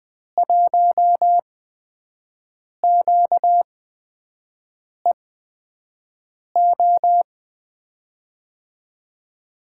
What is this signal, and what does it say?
Morse code, heard as a keyed tone, about 20 wpm, spaced as for 5 wpm, 700 Hz. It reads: 1QEO
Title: USA Call Signs - 6 Characters - 8x Character Spacing - 20wpm